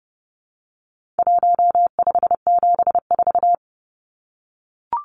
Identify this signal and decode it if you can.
Morse code, heard as a keyed tone, about 30 wpm, 700 Hz. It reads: 1574
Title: Numbers - 4-Digit 30wpm